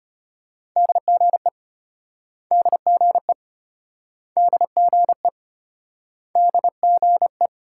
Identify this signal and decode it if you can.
Morse code, heard as a keyed tone, about 35 wpm, 700 Hz. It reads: DGE DGE DGE DGE